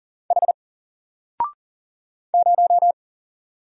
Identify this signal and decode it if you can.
Morse code, heard as a keyed tone, about 40 wpm, 700 Hz. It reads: H 0
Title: Single Letter-Number Alt 40wpm